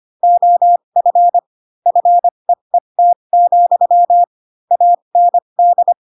Morse code, heard as a keyed tone, about 25 wpm, spaced as for 22 wpm, 700 Hz. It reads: OF FEET, AND
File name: Pinocchio / Pinocchio ch8 18 25 ik1hge